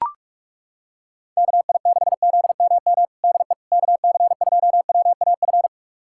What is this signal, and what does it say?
Morse code, heard as a keyed tone, about 42 wpm, 700 Hz. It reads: KI6ZMM DE KC2WAF
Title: USA Callsign DE Callsign 45wpm